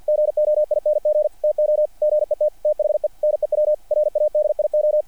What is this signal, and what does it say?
Morse code, heard as a keyed tone, about 50 wpm, 600 Hz, with radio noise. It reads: GOING TO GET THE NEW RADIO
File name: archived / CRSnet